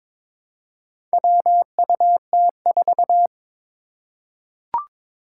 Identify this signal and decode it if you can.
Morse code, heard as a keyed tone, about 22 wpm, 700 Hz. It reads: WUT4
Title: Four Challenging Letter-Number Combinations 22wpm